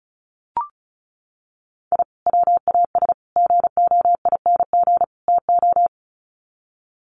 Morse code, heard as a keyed tone, about 34 wpm, 700 Hz. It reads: I WAS GOING TO
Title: Sets of 4 Words 35wpm